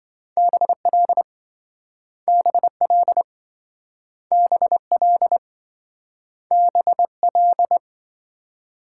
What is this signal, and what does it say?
Morse code, heard as a keyed tone, about 27 wpm, 700 Hz. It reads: BL BL BL BL